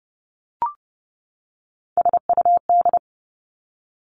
Morse code, heard as a keyed tone, about 30 wpm, 700 Hz. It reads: SUD
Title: Three Challenging Letter-Number Combinations 30wpm